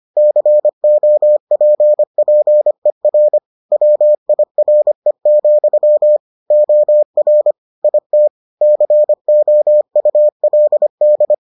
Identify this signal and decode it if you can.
Morse code, heard as a keyed tone, about 25 wpm, 600 Hz. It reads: COPPER WIRE, OR IT COULD